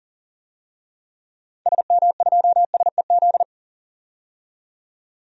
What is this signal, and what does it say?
Morse code, heard as a keyed tone, about 37 wpm, 700 Hz. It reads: SM2SEZ
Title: DXCC-100 Call Signs - 6 Characters - 40wpm